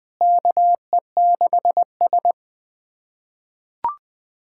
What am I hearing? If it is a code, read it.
Morse code, heard as a keyed tone, about 21 wpm, 700 Hz. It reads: KE6S
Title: USA Callsign DE Callsign - 4 Characters - 20wpm